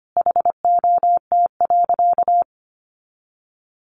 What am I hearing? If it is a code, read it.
Morse code, heard as a keyed tone, about 25 wpm, 700 Hz. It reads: HOT.
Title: Sentences from Top 500 Words 25wpm